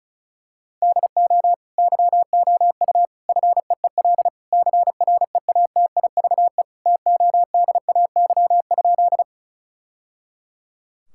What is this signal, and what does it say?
Morse code, heard as a keyed tone, about 34 wpm, 700 Hz. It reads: DO YOU FEEL CREATIVE TODAY?
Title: Sentences from Top 1000 Words - Review Part 5 - 35wpm